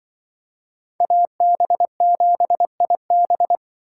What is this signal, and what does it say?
Morse code, heard as a keyed tone, about 24 wpm, 700 Hz. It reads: AB7IB
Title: Speed Racing - USA 2x2 Call Signs 20wpm